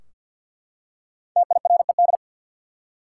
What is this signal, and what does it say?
Morse code, heard as a keyed tone, about 50 wpm, 700 Hz. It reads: TIRED
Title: Five Letter Words - 50wpm